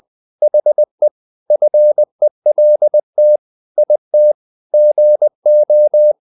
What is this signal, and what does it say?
Morse code, heard as a keyed tone, about 20 wpm, 600 Hz. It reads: HE FELT IT GO
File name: the old man and the sea part 2 20wpm